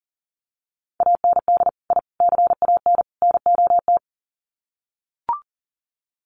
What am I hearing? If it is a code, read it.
Morse code, heard as a keyed tone, about 40 wpm, 700 Hz. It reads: AND I CAN NOT
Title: Sets of 4 Words 40wpm